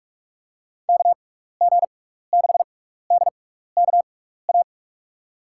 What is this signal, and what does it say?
Morse code, heard as a keyed tone, about 46 wpm, spaced as for 15 wpm, 700 Hz. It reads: KG6DKA
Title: USA Call Signs - 6 Characters - 6x Character Spacing - 45wpm